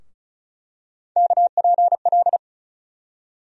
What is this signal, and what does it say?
Morse code, heard as a keyed tone, about 34 wpm, 700 Hz. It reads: KPL
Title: USA Call Sign Postfixes - 3 Letters 35wpm